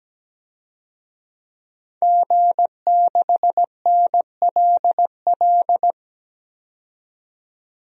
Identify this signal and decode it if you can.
Morse code, heard as a keyed tone, about 17 wpm, 700 Hz. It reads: G6NLL
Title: DXCC-100 Call Signs - 5 Characters - 17wpm